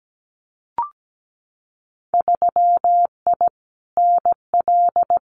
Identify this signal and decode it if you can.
Morse code, heard as a keyed tone, about 17 wpm, 700 Hz. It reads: 3I NL
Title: Winter Field Day Exchanges 17wpm